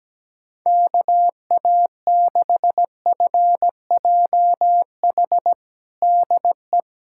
Morse code, heard as a keyed tone, about 17 wpm, 700 Hz. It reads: KA6FJH DE
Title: USA Callsign DE Callsign - 6 Characters - 17wpm